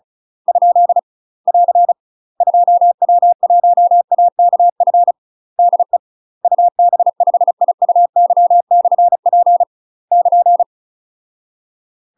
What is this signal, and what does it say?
Morse code, heard as a keyed tone, about 35 wpm, 700 Hz. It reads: ? <AR> 2W1AKF DE UB5IUY/P <KN>
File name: UK MASTER QSOs TRAINING / 35WPM / UK MASTER QSOs 0005